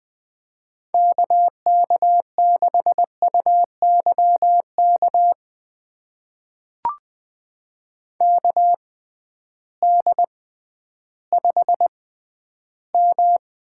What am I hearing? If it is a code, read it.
Morse code, heard as a keyed tone, about 20 wpm, spaced as for 12 wpm, 700 Hz. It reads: KK6UYK KD5M
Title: USA Call Signs - 6 Characters - 6x Character Spacing - 20wpm